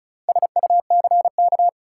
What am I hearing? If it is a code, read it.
Morse code, heard as a keyed tone, about 36 wpm, 700 Hz. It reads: SUCK
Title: Four Letter Words - 35wpm